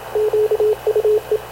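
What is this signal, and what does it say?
Morse code, heard as a keyed tone, about 27 wpm, 430 Hz, with radio noise. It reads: QUE
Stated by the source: Havana Radio / CLA sending Spanish press (PX) in Morse, recorded by Radio Officer Paul Dumesnil onboard CNAV QUEST/CZDO.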